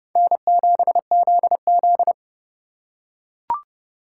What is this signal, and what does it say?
Morse code, heard as a keyed tone, about 30 wpm, 700 Hz. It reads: N7ZZ
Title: USA Call Signs - 4 Characters - 2x Character Spacing - 30wpm